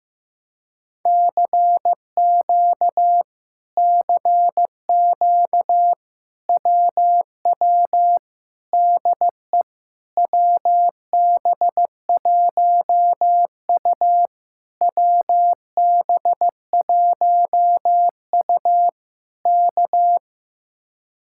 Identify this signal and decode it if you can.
Morse code, heard as a keyed tone, about 15 wpm, 700 Hz. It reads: CQ CQ WW DE WB1U WB1U K
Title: Speed-Racing Common QSO Phrases Encore - 15wpm